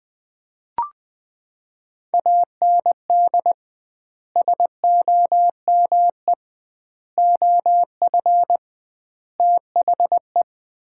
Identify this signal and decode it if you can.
Morse code, heard as a keyed tone, about 20 wpm, 700 Hz. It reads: AND SOME OF THE
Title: 2x Word-Spacing - Sets of 4 Words - 20wpm